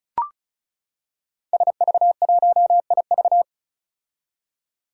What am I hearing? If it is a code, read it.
Morse code, heard as a keyed tone, about 36 wpm, 700 Hz. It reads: SV1IV
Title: Five Challenging Letter-Number Combinations 35wpm